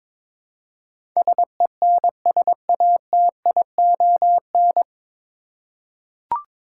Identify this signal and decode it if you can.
Morse code, heard as a keyed tone, about 22 wpm, 700 Hz. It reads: SENSATION
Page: Morse Code Ninja practice 22wpm